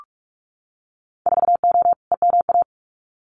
Morse code, heard as a keyed tone, about 46 wpm, 700 Hz. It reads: 4O EMA
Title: Winter Field Day Exchanges Encore 45wpm